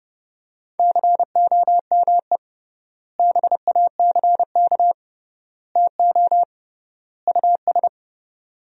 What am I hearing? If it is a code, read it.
Morse code, heard as a keyed tone, about 30 wpm, 700 Hz. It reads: COME BACK TO US
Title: 3x Word-Spacing - Sentences from Top 500 Words - 30wpm